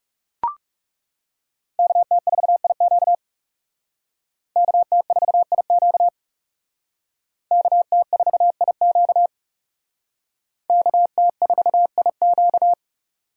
Morse code, heard as a keyed tone, about 37 wpm, 700 Hz. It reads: KT4IQ KT4IQ KT4IQ KT4IQ
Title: Speed Racing - USA 2x2 Call Signs 30wpm